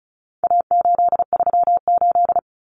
Morse code, heard as a keyed tone, about 34 wpm, 700 Hz. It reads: A838
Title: Common Cut Numbers - 4-Digit 35wpm